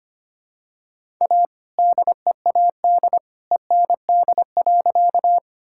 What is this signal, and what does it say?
Morse code, heard as a keyed tone, about 25 wpm, 700 Hz. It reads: A DEAD END.
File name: practice / 25wpm / All-Age Appropriate Jokes - 25wpm